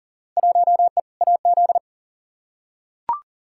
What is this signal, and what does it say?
Morse code, heard as a keyed tone, about 40 wpm, 700 Hz. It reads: 1E AZ
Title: Field Day Exchanges - 40wpm